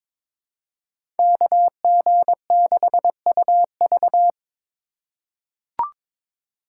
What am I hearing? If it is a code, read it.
Morse code, heard as a keyed tone, about 22 wpm, 700 Hz. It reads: KG6UV
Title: USA Call Signs - 5 Characters - 4x Character Spacing - 22wpm